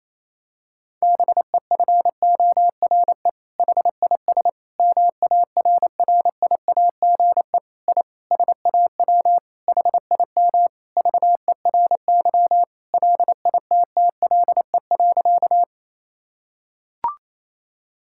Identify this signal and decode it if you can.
Morse code, heard as a keyed tone, about 28 wpm, 700 Hz. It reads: BEFORE HIS MARRIAGE I SAW HIM VERY LITTLE.
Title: Sentences from Top 1000 Words - Review Part 5 - 28wpm